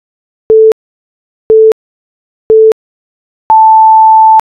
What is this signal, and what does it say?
Radio time signal (time pips). It is heard from 0.5 to 4.4 s.